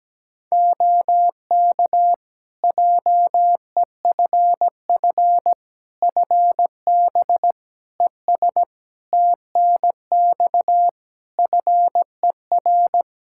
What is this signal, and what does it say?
Morse code, heard as a keyed tone, about 17 wpm, 700 Hz. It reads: OK JEFF FB ES TNX FER
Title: K1USN SST QSO Practice 17wpm